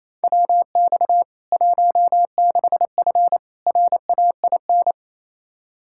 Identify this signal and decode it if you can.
Morse code, heard as a keyed tone, about 28 wpm, 700 Hz. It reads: WX 16F RAIN
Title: K1USN SST QSO Practice 28wpm